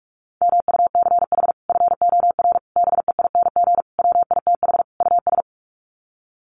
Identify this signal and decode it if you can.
Morse code, heard as a keyed tone, about 44 wpm, 700 Hz. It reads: MUCH FOR BEING WITH US
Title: Sets of 5 Words 45wpm